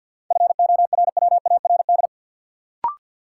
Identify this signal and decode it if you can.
Morse code, heard as a keyed tone, about 51 wpm, 700 Hz. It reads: FORWARD
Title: Sentences from Top 1300 Words - New Word Drill Alt - 50wpm